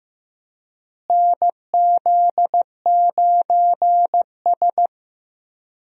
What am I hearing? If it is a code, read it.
Morse code, heard as a keyed tone, about 16 wpm, 700 Hz. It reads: NZ9S
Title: USA Call Signs - 4 Characters - Repeated 3x - 15wpm